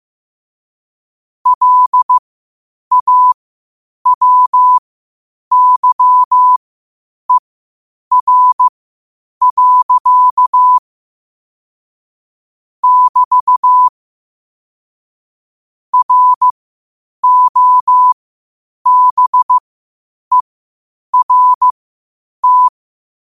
Morse code, heard as a keyed tone, about 15 wpm, spaced as for 8 wpm, 1000 Hz. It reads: LAWYER. <BT> ROBERT
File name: qotdfiles / 8wpm